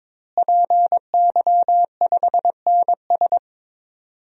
Morse code, heard as a keyed tone, about 22 wpm, 700 Hz. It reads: PY5NS
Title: DXCC-100 Call Signs - 5 Characters - Repeated 3x - 22wpm